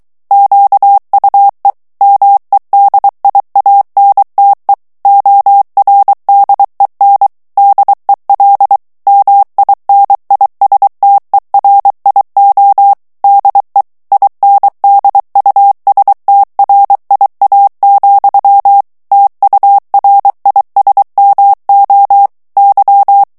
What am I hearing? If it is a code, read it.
Morse code, heard as a keyed tone, about 23 wpm, 800 Hz. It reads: QUE MEDIANTE ORDEN DEL MINISTERIO DE INDUSTRIA, TURISMO Y
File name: morse mp3 / ESPAGNOL TEXTE EN CLAIR 2 VITESSE 22 MOTS MINUTES